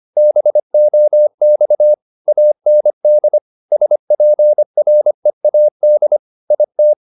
Morse code, heard as a keyed tone, about 25 wpm, 600 Hz. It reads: BOX AND SPREAD IT
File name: the old man and the sea part 1 25wpm